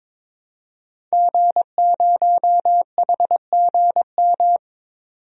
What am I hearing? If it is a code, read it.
Morse code, heard as a keyed tone, about 22 wpm, 700 Hz. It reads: G0HGM
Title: DXCC-100 Call Signs - 5 Characters - 22wpm